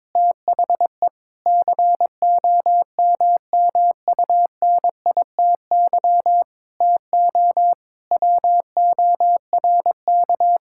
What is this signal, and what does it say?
Morse code, heard as a keyed tone, about 22 wpm, 700 Hz. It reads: THE COMMUNITY TO WORK
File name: Sentences with 7-Word Phrases - 22wpm